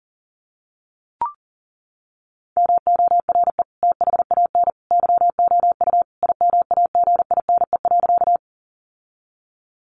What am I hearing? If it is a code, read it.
Morse code, heard as a keyed tone, about 40 wpm, 700 Hz. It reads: MORE THAN YOU IMAGINE.
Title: Sentences from Top 1400 Words 40wpm